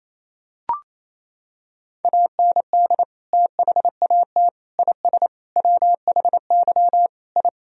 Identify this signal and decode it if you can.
Morse code, heard as a keyed tone, about 28 wpm, 700 Hz. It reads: AND THAT IS WHY I
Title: Sentences from Top 1000 Words - Review Part 3 - 28wpm